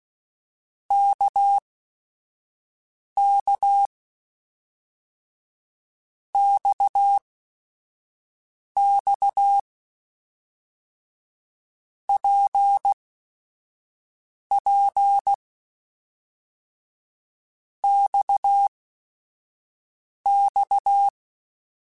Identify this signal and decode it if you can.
Morse code, heard as a keyed tone, about 16 wpm, spaced as for 5 wpm, 800 Hz. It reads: KK XX PP XX
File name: Basic Sets / Basic Letter Groups Collective7 / RKPX@5